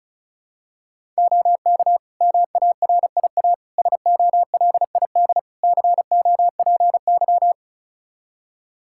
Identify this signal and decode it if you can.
Morse code, heard as a keyed tone, about 34 wpm, 700 Hz. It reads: OK MARIA SOLID COPY
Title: Practice QSO - Exchange 3 - More Info - 35wpm